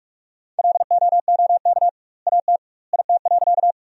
Morse code, heard as a keyed tone, about 43 wpm, 700 Hz. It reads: LOOK AT IT.
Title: 2x Word-Spacing - Sentences from Top 100 Words - 45wpm